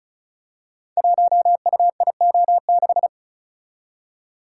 Morse code, heard as a keyed tone, about 36 wpm, 700 Hz. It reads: 1UIO6
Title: Code Groups - 5 Letter:Numbers - 35wpm